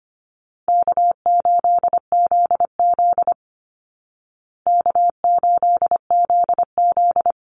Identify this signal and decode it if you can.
Morse code, heard as a keyed tone, about 25 wpm, 700 Hz. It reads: K8ZZ K8ZZ
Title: Call Signs Repeated 3x - 25wpm